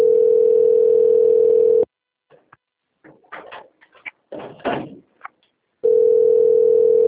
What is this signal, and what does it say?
Ringback tone (North America), 440 + 480 Hz.